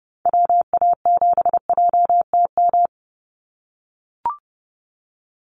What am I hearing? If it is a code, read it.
Morse code, heard as a keyed tone, about 30 wpm, 700 Hz. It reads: WA7JTM
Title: NA SOTA Activator Callsigns 30wpm